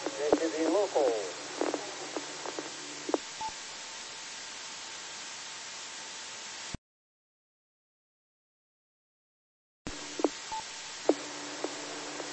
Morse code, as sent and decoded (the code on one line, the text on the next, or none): none